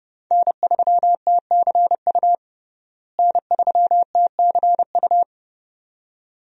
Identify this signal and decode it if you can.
Morse code, heard as a keyed tone, about 30 wpm, 700 Hz. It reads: N3TCU N3TCU
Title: USA Call Signs - 5 Characters - Repeated 2x - 30wpm